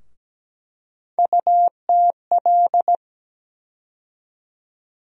Morse code, heard as a keyed tone, about 17 wpm, 700 Hz. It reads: UTL
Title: USA Call Sign Postfixes - 3 Letters 17wpm